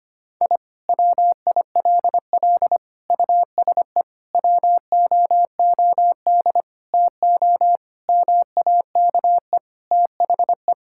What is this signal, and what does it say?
Morse code, heard as a keyed tone, about 25 wpm, 700 Hz. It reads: I WILL USE WOOD TO MAKE THE
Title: Sentences from Top 1000 Words - Review Part 6 - 25wpm